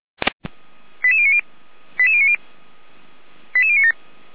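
Five-tone selective call (ZVEI), heard as quick runs of five tones, tones of about 80 ms. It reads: 88090 88090 88097
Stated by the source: ZVEI five-tone selective call received with an SDR.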